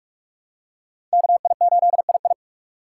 Morse code, heard as a keyed tone, about 43 wpm, 700 Hz. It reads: KI8II
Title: USA Call Signs - 5 Characters - 6x Character Spacing - 45wpm